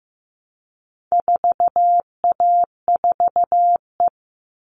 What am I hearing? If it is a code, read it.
Morse code, heard as a keyed tone, about 15 wpm, 700 Hz. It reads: 4A4E
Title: All Cut Numbers - 4-Digit 15wpm